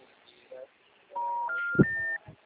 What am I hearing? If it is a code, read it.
Special information tone, with speech.
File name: out- -2002- - - .139